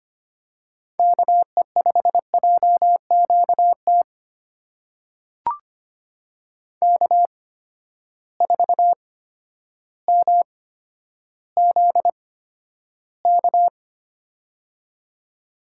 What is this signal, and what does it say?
Morse code, heard as a keyed tone, about 25 wpm, spaced as for 11 wpm, 700 Hz. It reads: KE5JQT K4MZK
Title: USA Call Signs - 8x Character Spacing - 25wpm